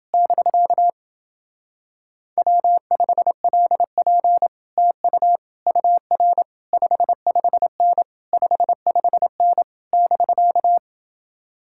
Morse code, heard as a keyed tone, about 30 wpm, 700 Hz. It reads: <BK> W5LP TU UR 55N 55N <BK>